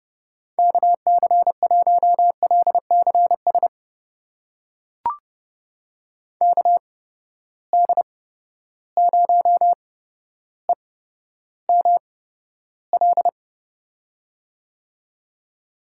Morse code, heard as a keyed tone, about 30 wpm, spaced as for 13 wpm, 700 Hz. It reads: KC1LCS KD0EML
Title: USA Call Signs - 6 Characters - 8x Character Spacing - 30wpm